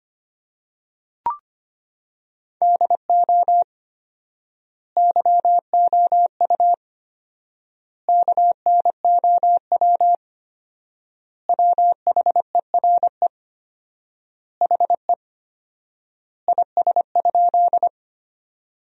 Morse code, heard as a keyed tone, about 25 wpm, 700 Hz. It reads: DO YOU KNOW WHERE HE IS?
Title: 4x Word-Spacing - Sentences from Top 300 Words - 25wpm